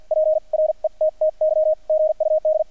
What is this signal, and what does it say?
Morse code, heard as a keyed tone, about 50 wpm, spaced as for 41 wpm, 620 Hz, with radio noise. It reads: PRETTY GUD
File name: archived / CRSnet